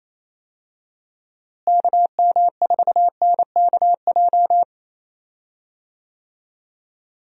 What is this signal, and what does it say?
Morse code, heard as a keyed tone, about 28 wpm, 700 Hz. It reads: KM4NKJ
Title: USA Call Signs - 6 Characters - Repeated 2x - 28wpm